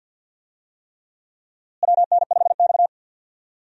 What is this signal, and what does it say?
Morse code, heard as a keyed tone, about 52 wpm, 700 Hz. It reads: WN5X
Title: USA Call Signs - 4 Characters - Repeated 3x - 50wpm